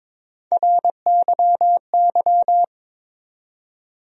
Morse code, heard as a keyed tone, about 22 wpm, 700 Hz. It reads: RYY
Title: USA Call Sign Postfixes - 3 Letters 22wpm